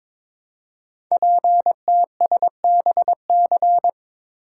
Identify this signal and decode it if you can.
Morse code, heard as a keyed tone, about 22 wpm, 700 Hz. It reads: PTSBC
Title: Code Groups - 5 Letters - 22wpm